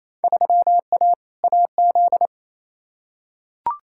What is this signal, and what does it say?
Morse code, heard as a keyed tone, about 28 wpm, 700 Hz. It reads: 3A AZ
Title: Field Day Exchanges Encore - 28wpm